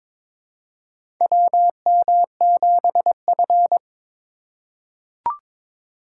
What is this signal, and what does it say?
Morse code, heard as a keyed tone, about 22 wpm, 700 Hz. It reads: WM7F
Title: USA Call Signs - 4 Characters - 3x Character Spacing - 22wpm